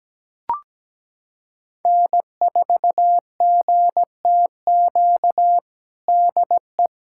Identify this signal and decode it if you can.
Morse code, heard as a keyed tone, about 17 wpm, 700 Hz. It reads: N4GTQ DE
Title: USA Callsign DE Callsign - 5 Characters - 17wpm